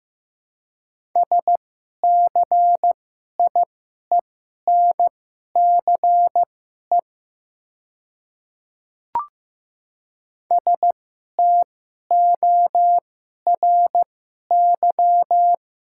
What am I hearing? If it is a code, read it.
Morse code, heard as a keyed tone, about 15 wpm, spaced as for 11 wpm, 690 Hz. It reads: SCIENCE STORY
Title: 2x Character Spacing - Top 500 Words - 15wpm